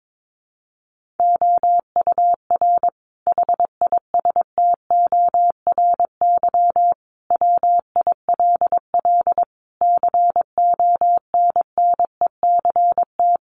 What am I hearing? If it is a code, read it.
Morse code, heard as a keyed tone, about 22 wpm, 700 Hz. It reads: OUR HISTORY WILL CONNECT
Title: Sentences from Top 1500 Words 22wpm